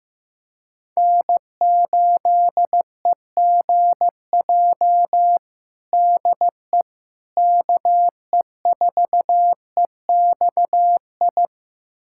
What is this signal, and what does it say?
Morse code, heard as a keyed tone, about 15 wpm, 700 Hz. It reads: N8EGJ DE KE4EXI
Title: USA Callsign DE Callsign 15wpm